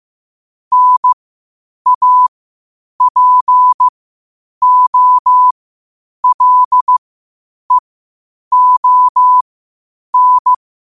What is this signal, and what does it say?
Morse code, heard as a keyed tone, about 15 wpm, spaced as for 8 wpm, 1000 Hz. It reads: NAPOLEON